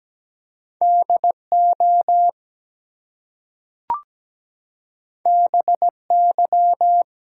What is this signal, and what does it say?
Morse code, heard as a keyed tone, about 17 wpm, 700 Hz. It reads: DO BY
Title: Two Letter Words - 17wpm